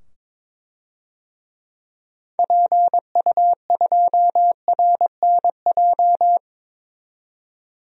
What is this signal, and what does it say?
Morse code, heard as a keyed tone, about 22 wpm, 700 Hz. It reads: PU2RNJ
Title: DXCC-100 Call Signs - 6 Characters - 22wpm